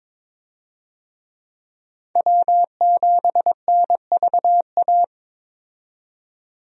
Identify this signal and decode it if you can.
Morse code, heard as a keyed tone, about 22 wpm, 700 Hz. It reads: W7NVA
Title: USA Call Signs - 5 Characters - 22wpm